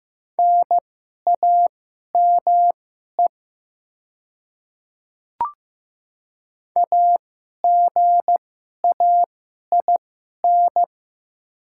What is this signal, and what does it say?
Morse code, heard as a keyed tone, about 15 wpm, spaced as for 11 wpm, 700 Hz. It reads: NAME AGAIN
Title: 2x Character Spacing - Top 200 Words - 15wpm